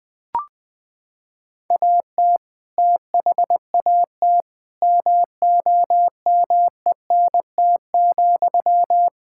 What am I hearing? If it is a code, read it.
Morse code, heard as a keyed tone, about 21 wpm, 700 Hz. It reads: AT THAT MOMENT,
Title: Sentences from Top 1000 Words - Review Part 5 - 20wpm